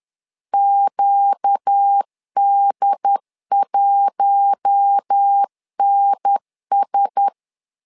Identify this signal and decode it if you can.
Morse code, heard as a keyed tone, about 10 wpm, 800 Hz. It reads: QD1NS
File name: LEZIONI / 4 Q1S 10wpm 2133